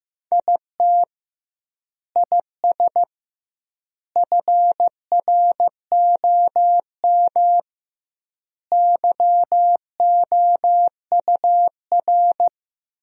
Morse code, heard as a keyed tone, about 15 wpm, 700 Hz. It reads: IT IS FROM YOUR
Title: 2x Word-Spacing - Sentences from Top 300 Words - 15wpm